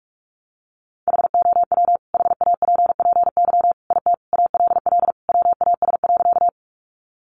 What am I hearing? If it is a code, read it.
Morse code, heard as a keyed tone, about 44 wpm, 700 Hz. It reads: HOW HAPPY IT ALL WAS.
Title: Sentences from Top 1000 Words 45wpm